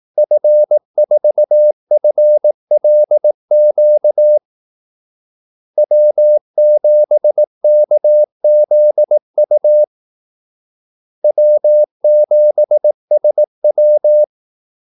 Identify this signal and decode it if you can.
Morse code, heard as a keyed tone, about 18 wpm, 600 Hz. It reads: F4FLQ W7KZU W7SW
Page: Kaldesignaler 18 wpm